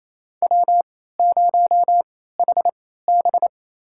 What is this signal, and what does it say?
Morse code, heard as a keyed tone, about 28 wpm, spaced as for 16 wpm, 700 Hz. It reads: W0HB
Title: USA Call Signs - 4 Characters - 3x Character Spacing - 28wpm